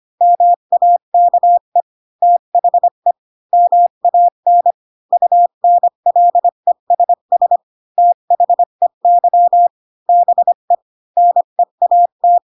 Morse code, heard as a keyed tone, about 25 wpm, 700 Hz. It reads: MAKE THE MAN UNLESS THEY BE NEAT
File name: Pinocchio / Pinocchio ch8 20 25 ik1hge